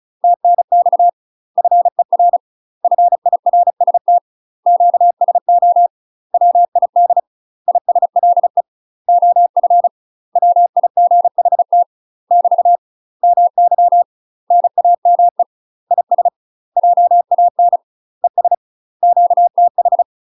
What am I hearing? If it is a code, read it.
Morse code, heard as a keyed tone, about 34 wpm, 700 Hz. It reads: TNX FER FIRST QSO WID ISLE OF WIGHT <BT> MY NAME IS JAN ES QTH
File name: UK MASTER QSOs TRAINING / 35WPM / UK MASTER QSOs 0021